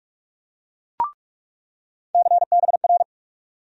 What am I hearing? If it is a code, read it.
Morse code, heard as a keyed tone, about 46 wpm, 700 Hz. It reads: CBR
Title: USA Call Sign Postfixes - 3 Letters 45wpm